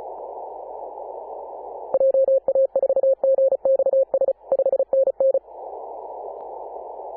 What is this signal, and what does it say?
Morse code, heard as a keyed tone, about 34 wpm, 540 Hz. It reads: JA4GXS 5NN
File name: Apr/29 2042z JD1BLY 14.011MHz CW